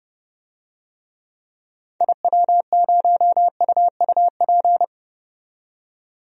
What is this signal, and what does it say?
Morse code, heard as a keyed tone, about 30 wpm, 700 Hz. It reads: IW0UUP
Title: DXCC-100 Call Signs - 6 Characters - 30wpm